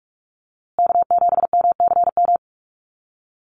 Morse code, heard as a keyed tone, about 44 wpm, 700 Hz. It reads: K7MCM
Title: USA Call Signs Repeated 3x 45wpm